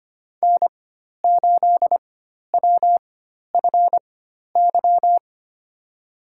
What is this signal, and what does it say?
Morse code, heard as a keyed tone, about 25 wpm, spaced as for 12 wpm, 700 Hz. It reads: N8WFY
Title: USA Call Signs - 5 Characters - 4x Character Spacing - 25wpm